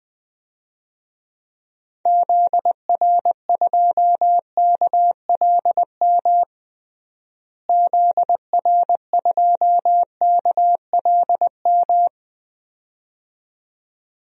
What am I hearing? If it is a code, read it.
Morse code, heard as a keyed tone, about 20 wpm, 700 Hz. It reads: ZR2KLM ZR2KLM
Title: DXCC-100 Call Signs - 6 Characters - Repeated 2x - 20wpm